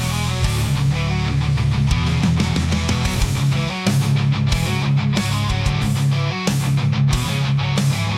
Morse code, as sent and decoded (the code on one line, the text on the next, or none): none